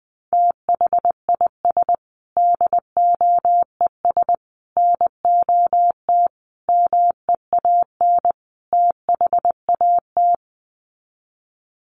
Morse code, heard as a keyed tone, about 20 wpm, 700 Hz. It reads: THIS DOES NOT MEAN THAT
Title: Sets of 5 Words Encore2 20wpm